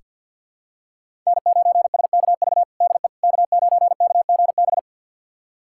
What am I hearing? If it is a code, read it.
Morse code, heard as a keyed tone, about 47 wpm, 700 Hz. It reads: N9SKV DE K9KGB
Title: USA Callsign DE Callsign - 5 Characters - 50wpm